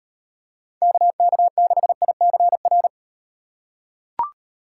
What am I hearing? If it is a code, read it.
Morse code, heard as a keyed tone, about 37 wpm, 700 Hz. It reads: KK6ICR